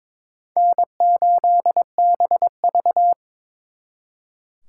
Morse code, heard as a keyed tone, about 26 wpm, 700 Hz. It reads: N8BV
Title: Speed Racing - USA 1x2 Call Signs 22wpm